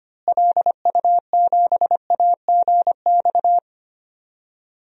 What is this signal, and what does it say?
Morse code, heard as a keyed tone, about 25 wpm, 700 Hz. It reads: LU7AGX
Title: DXCC-100 Call Signs - 6 Characters - Repeated 3x - 25wpm